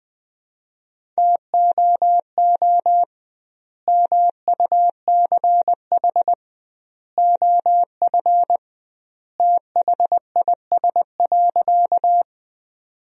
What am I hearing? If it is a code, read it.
Morse code, heard as a keyed tone, about 20 wpm, 700 Hz. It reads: TOO MUCH OF THIS.
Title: 2x Word-Spacing - Sentences from Top 200 Words - 20wpm